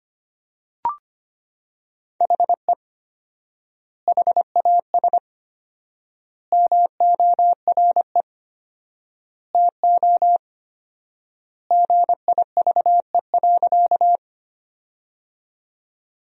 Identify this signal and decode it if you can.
Morse code, heard as a keyed tone, about 25 wpm, 700 Hz. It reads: HE HAS MORE TO GIVE.
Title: 4x Word-Spacing - Sentences from Top 200 Words - 25wpm